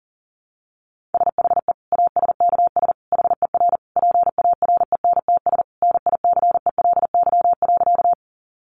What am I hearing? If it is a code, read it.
Morse code, heard as a keyed tone, about 40 wpm, 700 Hz. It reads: SHE ASKS HER PARENTS NICELY.
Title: Easy Phrases 40wpm